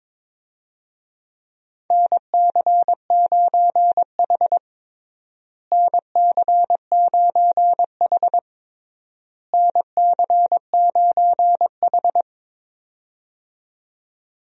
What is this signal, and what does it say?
Morse code, heard as a keyed tone, about 22 wpm, 700 Hz. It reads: NC9H NC9H NC9H
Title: USA Call Signs - 4 Characters - Repeated 3x - 22wpm